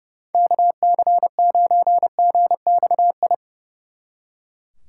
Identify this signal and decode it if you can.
Morse code, heard as a keyed tone, about 35 wpm, 700 Hz. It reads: KC9GXI
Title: Speed Racing - USA Call Signs 30wpm